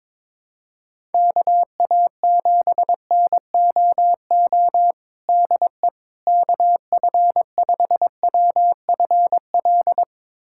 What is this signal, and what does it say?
Morse code, heard as a keyed tone, about 22 wpm, 700 Hz. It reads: KA7NOO DE KF5WFL
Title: USA Callsign DE Callsign 22wpm